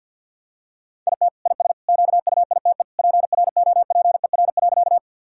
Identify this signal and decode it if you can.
Morse code, heard as a keyed tone, about 47 wpm, 700 Hz. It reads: IT IS QUITE PROPER.
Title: Sentences from Top 1000 Words - Review Part 5 - 50wpm